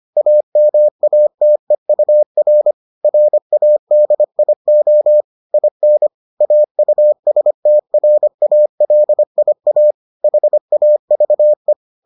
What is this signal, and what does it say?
Morse code, heard as a keyed tone, about 25 wpm, 600 Hz. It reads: AMATEUR RADIO IN AUSTRALIA HAVE